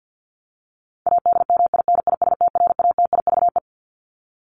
Morse code, heard as a keyed tone, about 48 wpm, 700 Hz. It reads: ADMINISTRATIVE
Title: Top 1501-1750 Words 50wpm